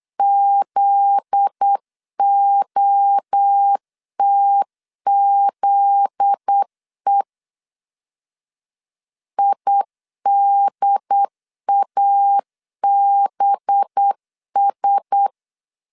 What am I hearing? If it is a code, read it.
Morse code, heard as a keyed tone, about 8 wpm, 800 Hz. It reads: ZOTZE IDABS
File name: LEZIONI / 7 GF9Z / 7 GF9Z 8wpm 3213